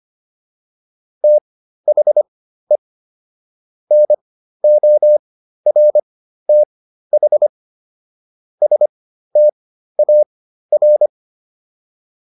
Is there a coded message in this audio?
Yes: Morse code from 1.2 to 11.1 s, heard as a keyed tone, about 25 wpm, spaced as for 13 wpm, 600 Hz. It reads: THE NORTH STAR